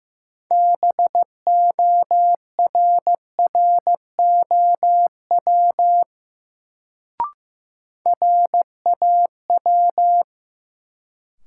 Morse code, heard as a keyed tone, about 15 wpm, 700 Hz. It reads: BORROW RAW
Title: Sentences from Top 1500 Words - New Word Drill - 15wpm